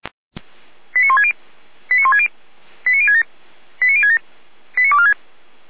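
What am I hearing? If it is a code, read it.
Five-tone selective call (ZVEI), heard as quick runs of five tones, tones of about 70 ms. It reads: 89160 89160 89067 89067 89246
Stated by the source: ZVEI five-tone selective call 89169 received by SDR.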